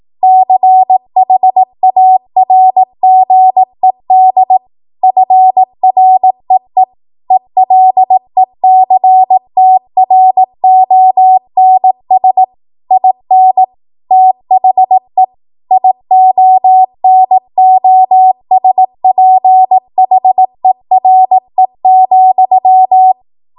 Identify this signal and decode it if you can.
Morse code, heard as a keyed tone, about 18 wpm, 750 Hz. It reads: CHARGED FREE ELECTRONS IN THE IONOSPHERE,
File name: Morse / 18WPM